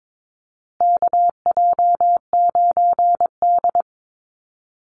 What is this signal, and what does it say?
Morse code, heard as a keyed tone, about 22 wpm, 700 Hz. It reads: KJ9D
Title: USA Call Signs Repeated 3x 22wpm